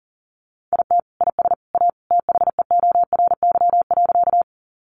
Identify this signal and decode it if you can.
Morse code, heard as a keyed tone, about 40 wpm, 700 Hz. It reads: IT IS A THEORY.
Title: Sentences from Top 600 Words 40wpm